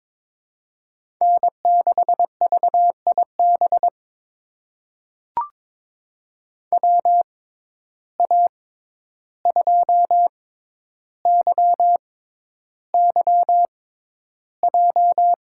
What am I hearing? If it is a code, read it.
Morse code, heard as a keyed tone, about 22 wpm, spaced as for 11 wpm, 700 Hz. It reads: N6VIB WA2YYJ
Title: USA Call Signs - 6x Character Spacing - 22wpm